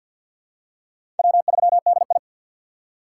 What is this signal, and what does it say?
Morse code, heard as a keyed tone, about 53 wpm, 690 Hz. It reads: W3DI
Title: USA Call Signs - 2x Character Spacing - 50wpm